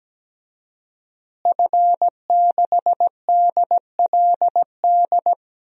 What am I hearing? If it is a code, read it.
Morse code, heard as a keyed tone, about 17 wpm, 700 Hz. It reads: F6DLD
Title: DXCC-100 Call Signs - 5 Characters - Repeated 3x - 17wpm